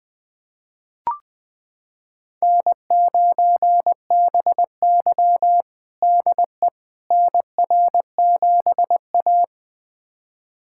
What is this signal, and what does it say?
Morse code, heard as a keyed tone, about 20 wpm, 700 Hz. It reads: N9BY DE NR7A
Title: USA Callsign DE Callsign - 4 Characters - 20wpm